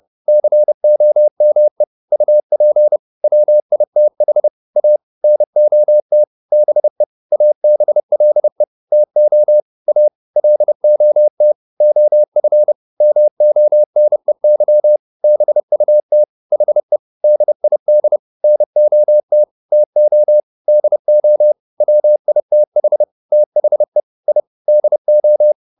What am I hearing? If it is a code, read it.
Morse code, heard as a keyed tone, about 30 wpm, 600 Hz. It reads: COME UP WITH A NOT BE ABLE TO A LOT OF MONEY BUT HE DID NOT TO DO WITH THE I DO
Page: advanced-practice-files